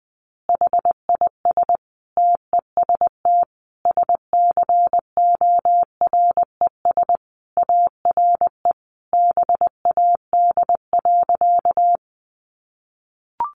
Morse code, heard as a keyed tone, about 20 wpm, 700 Hz. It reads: HIS TEST SCORES ARE BAD.
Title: Additional Easy Phrases 20wpm